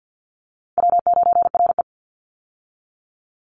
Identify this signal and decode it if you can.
Morse code, heard as a keyed tone, about 50 wpm, 700 Hz. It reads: W9RE
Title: Easy Call Signs - 50wpm